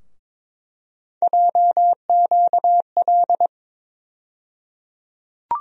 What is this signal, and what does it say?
Morse code, heard as a keyed tone, about 22 wpm, 700 Hz. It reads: JQL
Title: USA Call Sign Postfixes - 3 Letters 22wpm